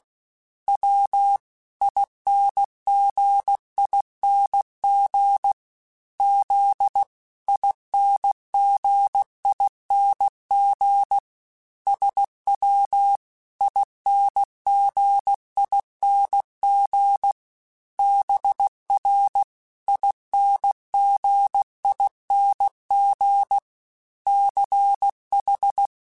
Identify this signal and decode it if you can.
Morse code, heard as a keyed tone, about 16 wpm, 800 Hz. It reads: W INGING Z INGING SW INGING BR INGING CH
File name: Special INGs-mp3s / Special INGs-mp3s / Dble Ings sp @15